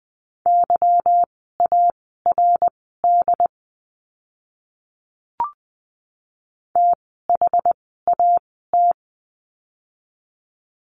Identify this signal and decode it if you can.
Morse code, heard as a keyed tone, about 20 wpm, spaced as for 15 wpm, 700 Hz. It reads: YARD THAT
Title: 146 Lesson 24f - Letter y - Words 20wpm